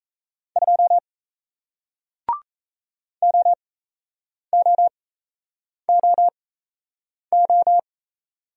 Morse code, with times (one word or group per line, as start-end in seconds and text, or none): 0.6-1.0 2
3.2-7.8 OOOO